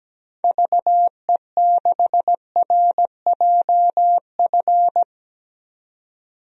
Morse code, heard as a keyed tone, about 17 wpm, 700 Hz. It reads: VE6RJF
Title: DXCC-100 Call Signs - 6 Characters - 17wpm